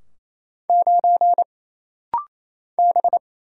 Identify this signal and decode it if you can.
Morse code, heard as a keyed tone, about 27 wpm, 700 Hz. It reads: 9 B
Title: Single Letter-Number - Rapid-Fire 28wpm